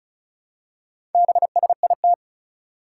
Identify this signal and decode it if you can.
Morse code, heard as a keyed tone, about 36 wpm, 700 Hz. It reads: BSIT
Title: Four Challenging Letter-Number Combinations 35wpm